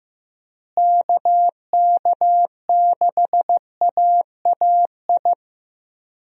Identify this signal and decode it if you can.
Morse code, heard as a keyed tone, about 16 wpm, 700 Hz. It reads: KK6AAI
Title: USA Call Signs - 6 Characters - Repeated 3x - 15wpm